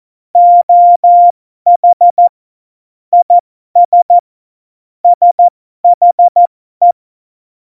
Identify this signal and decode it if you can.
Morse code, heard as a keyed tone, about 14 wpm, spaced as for 12 wpm, 700 Hz. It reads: OH IS SHE
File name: Morsekurs / Witze